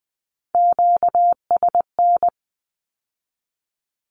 Morse code, heard as a keyed tone, about 20 wpm, 700 Hz. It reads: QSN
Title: Q-Codes with Spoken Meaning - 20wpm